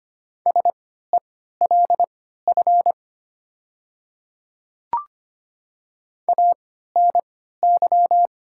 Morse code, heard as a keyed tone, about 25 wpm, spaced as for 14 wpm, 700 Hz. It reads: SELF ANY
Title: 3x Character Spacing - Top 200 Words - 25wpm